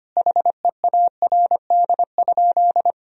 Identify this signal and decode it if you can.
Morse code, heard as a keyed tone, about 25 wpm, 700 Hz. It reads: HEARD?
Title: Sentences from Top 400 Words and Abbreviations 25wpm